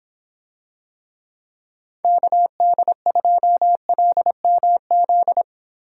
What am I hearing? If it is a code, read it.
Morse code, heard as a keyed tone, about 26 wpm, 700 Hz. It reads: KD2LMZ